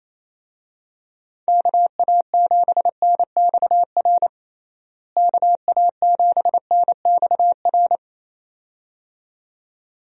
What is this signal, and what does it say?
Morse code, heard as a keyed tone, about 28 wpm, 700 Hz. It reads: KA7NXR KA7NXR
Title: USA Call Signs - 6 Characters - Repeated 2x - 28wpm